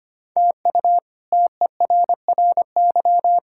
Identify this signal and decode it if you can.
Morse code, heard as a keyed tone, about 25 wpm, 700 Hz. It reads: TU TERRY
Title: K1USN SST QSO Practice - 25wpm